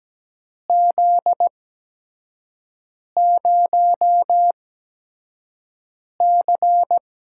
Morse code, heard as a keyed tone, about 17 wpm, spaced as for 5 wpm, 700 Hz. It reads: Z0C
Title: USA Call Signs - 8x Character Spacing - 17wpm